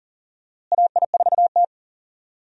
Morse code, heard as a keyed tone, about 40 wpm, 700 Hz. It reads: AI4T
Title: USA Call Signs - 4 Characters - Repeated 2x - 40wpm